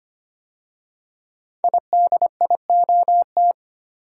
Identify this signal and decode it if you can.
Morse code, heard as a keyed tone, about 25 wpm, 700 Hz. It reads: IDIOT
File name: 5 Letter Words - First 3 Letters Given - 25wpm